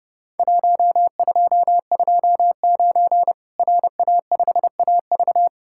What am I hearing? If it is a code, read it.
Morse code, heard as a keyed tone, about 30 wpm, 700 Hz. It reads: 1229 RA5AV
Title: Worked All Europe DX Contest - QTC Exchange - 30wpm